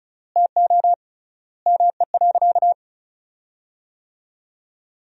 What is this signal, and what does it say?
Morse code, heard as a keyed tone, about 36 wpm, 700 Hz. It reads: TO ME.
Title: 3x Word-Spacing - Sentences from Top 500 Words - 35wpm